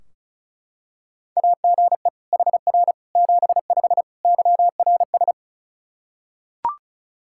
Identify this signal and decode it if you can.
Morse code, heard as a keyed tone, about 36 wpm, 700 Hz. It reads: AGE HR 75 YRS
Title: K1USN SST QSO Practice 35wpm